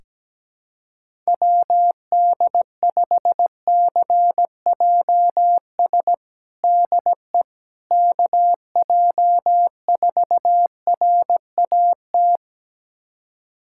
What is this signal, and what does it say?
Morse code, heard as a keyed tone, about 17 wpm, 700 Hz. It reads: WD5CJS DE KJ4RAT
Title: USA Callsign DE Callsign 17wpm